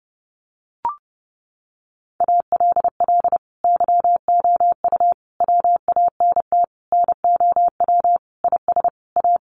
Morse code, heard as a keyed tone, about 30 wpm, 700 Hz. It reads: ALL YOU WANT NOW IS A
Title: Sentences from Top 1200 Words 30wpm